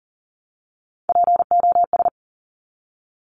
Morse code, heard as a keyed tone, about 39 wpm, 700 Hz. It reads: POS
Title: Top Three Letter Combinations in Words 40wpm